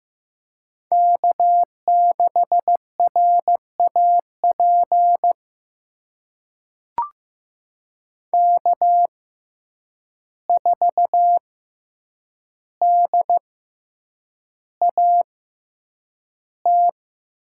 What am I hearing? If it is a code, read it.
Morse code, heard as a keyed tone, about 15 wpm, spaced as for 8 wpm, 700 Hz. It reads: K6RAP K4DAT
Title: USA Call Signs - 5 Characters - 6x Character Spacing - 15wpm